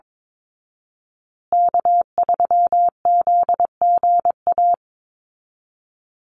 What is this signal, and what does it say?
Morse code, heard as a keyed tone, about 22 wpm, 700 Hz. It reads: K3ZGA
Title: Call Signs Repeated 3x - 22wpm